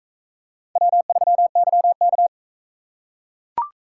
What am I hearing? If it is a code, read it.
Morse code, heard as a keyed tone, about 44 wpm, 700 Hz. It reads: W3YK